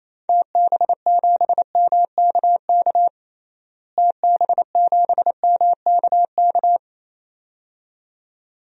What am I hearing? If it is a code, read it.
Morse code, heard as a keyed tone, about 28 wpm, 700 Hz. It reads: TB7MKK TB7MKK
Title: DXCC-100 Call Signs - 6 Characters - Repeated 2x - 28wpm